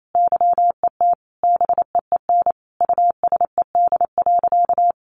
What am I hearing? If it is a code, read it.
Morse code, heard as a keyed tone, about 28 wpm, 700 Hz. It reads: YET BEEN USED.
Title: Sentences from Top 700 Words 28wpm